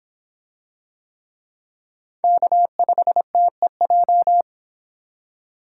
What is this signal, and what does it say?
Morse code, heard as a keyed tone, about 26 wpm, 700 Hz. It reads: K5TEJ